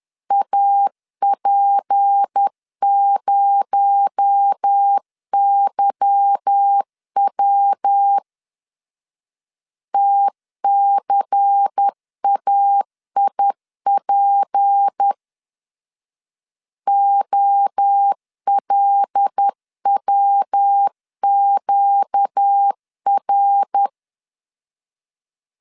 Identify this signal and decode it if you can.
Morse code, heard as a keyed tone, about 10 wpm, 800 Hz. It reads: AP0YW TCAIP OLWQR
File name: CW / DET / 1 9 1500C S A 10wpm 4511